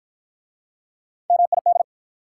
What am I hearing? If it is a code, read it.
Morse code, heard as a keyed tone, about 54 wpm, 700 Hz. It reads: MID